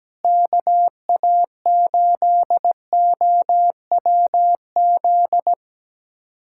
Morse code, heard as a keyed tone, about 17 wpm, 700 Hz. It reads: KA8OWZ
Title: USA Callsign DE Callsign - 6 Characters - 17wpm